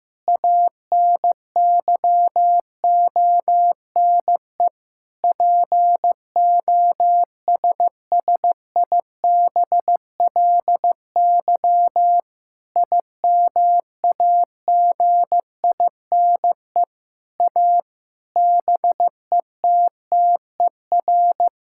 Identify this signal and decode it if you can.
Morse code, heard as a keyed tone, about 14 wpm, 700 Hz. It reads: ANYONE POSSIBLY IMAGINE A BETTER
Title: Sentences from Top 1600 Words 15wpm